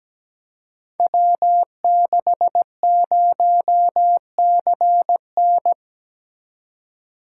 Morse code, heard as a keyed tone, about 17 wpm, 700 Hz. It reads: W60CN
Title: Code Groups - 5 Letter:Numbers - 17wpm